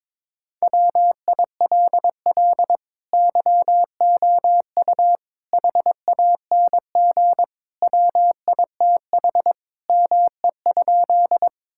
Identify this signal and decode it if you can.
Morse code, heard as a keyed tone, about 22 wpm, 700 Hz. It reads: WILL YOU HANG WITH ME?
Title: Sentences from Top 1600 Words 22wpm